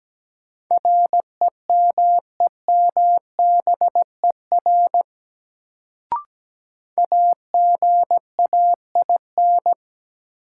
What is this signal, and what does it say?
Morse code, heard as a keyed tone, about 17 wpm, 700 Hz. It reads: REMEMBER AGAIN
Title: Sets of 4 Words - New Word Drill - 17wpm